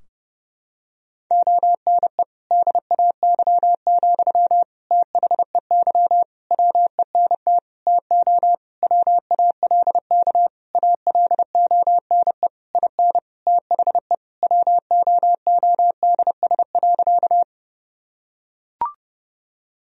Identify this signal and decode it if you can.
Morse code, heard as a keyed tone, about 30 wpm, 700 Hz. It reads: ONE DAY, THEY WENT TO WALK ALONE IN THE WOODS.
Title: Sentences from Top 1000 Words - Review Part 2 - 30wpm